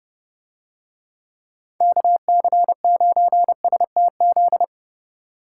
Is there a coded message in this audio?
Yes: Morse code from 1.8 to 4.6 s, heard as a keyed tone, about 29 wpm, 700 Hz. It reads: KC9STZ